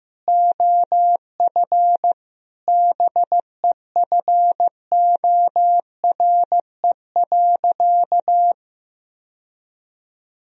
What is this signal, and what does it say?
Morse code, heard as a keyed tone, about 15 wpm, 700 Hz. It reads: OF BEFORE.
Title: Sentences from Top 1000 Words - Review Part 3 - 15wpm